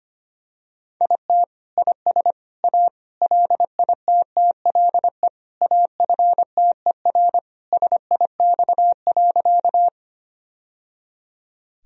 Morse code, heard as a keyed tone, about 25 wpm, 700 Hz. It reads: IT IS A LITTLE AFTER SIX.
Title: Sentences from Top 1000 Words - Review Part 2 - 25wpm